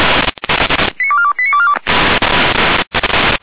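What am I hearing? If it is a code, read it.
Five-tone selective call (ZVEI), heard as quick runs of five tones, tones of about 70 ms. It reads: 89242 89242
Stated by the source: ZVEI five-tone selective call received via SDR.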